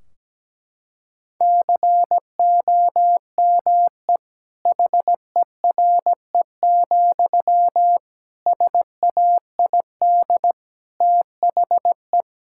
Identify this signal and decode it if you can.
Morse code, heard as a keyed tone, about 17 wpm, 700 Hz. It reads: COME HERE, SAID THE
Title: Sentences from Top 1600 Words 17wpm